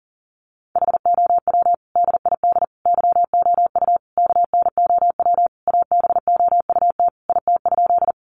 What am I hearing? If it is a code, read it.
Morse code, heard as a keyed tone, about 40 wpm, 700 Hz. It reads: HOW DID YOU KNOW ABOUT IT?
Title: Sentences from Top 100 Words 40wpm